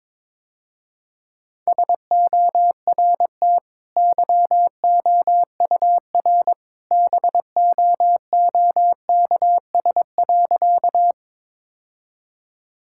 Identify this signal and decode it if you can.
Morse code, heard as a keyed tone, about 22 wpm, 700 Hz. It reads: SORT YOUR BOOKS.
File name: practice / 22wpm / Easy Sentences with Hints - 22wpm